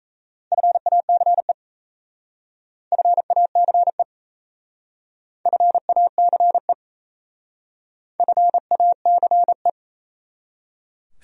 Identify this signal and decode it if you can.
Morse code, heard as a keyed tone, about 36 wpm, 700 Hz. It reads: FACE FACE FACE FACE